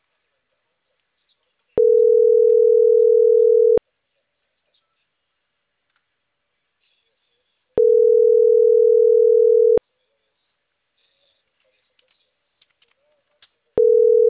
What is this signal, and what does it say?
Ringback tone (North America), 440 + 480 Hz.